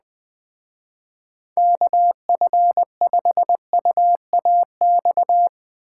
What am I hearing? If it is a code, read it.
Morse code, heard as a keyed tone, about 21 wpm, 700 Hz. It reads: KF5UAX
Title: Speed Racing - USA 2x3 Call Signs 17wpm